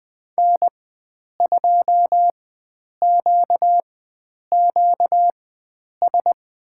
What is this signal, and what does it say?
Morse code, heard as a keyed tone, about 20 wpm, spaced as for 9 wpm, 700 Hz. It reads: N2QQS
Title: USA Call Signs - 5 Characters - 4x Character Spacing - 20wpm